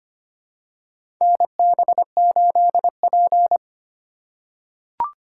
Morse code, heard as a keyed tone, about 25 wpm, 700 Hz. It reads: NB8P
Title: USA Call Signs - 4 Characters - 4x Character Spacing - 25wpm